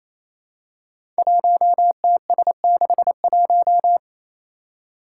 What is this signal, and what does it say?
Morse code, heard as a keyed tone, about 29 wpm, 700 Hz. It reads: 1TS61
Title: Five Challenging Letter-Number Combinations 28wpm